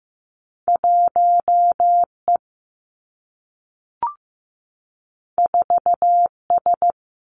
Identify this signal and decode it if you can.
Morse code, heard as a keyed tone, about 15 wpm, 700 Hz. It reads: 1 E 4 S
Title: Two Challenging Letter-Number Combinations 15wpm